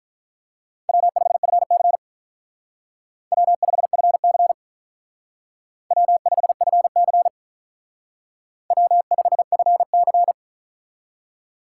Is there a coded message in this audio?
Yes: Morse code from 0.9 to 10.3 s, heard as a keyed tone, about 46 wpm, 700 Hz. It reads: W5FC W5FC W5FC W5FC